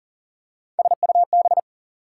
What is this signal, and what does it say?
Morse code, heard as a keyed tone, about 40 wpm, 700 Hz. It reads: SUB
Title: Prefixes and Suffixes 40wpm